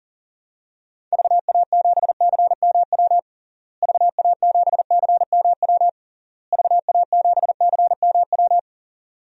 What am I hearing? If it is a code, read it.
Morse code, heard as a keyed tone, about 37 wpm, 700 Hz. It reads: VA7CMW VA7CMW VA7CMW
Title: DXCC-100 Call Signs - 6 Characters - Repeated 3x - 40wpm